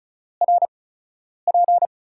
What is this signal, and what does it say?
Morse code, heard as a keyed tone, about 36 wpm, spaced as for 10 wpm, 700 Hz. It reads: RP
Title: USA Call Signs - 4 Characters - 8x Character Spacing - 35wpm